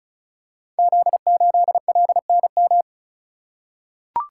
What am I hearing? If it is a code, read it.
Morse code, heard as a keyed tone, about 36 wpm, 690 Hz. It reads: Z8LNM
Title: Code Groups - 5 Letter:Numbers - 35wpm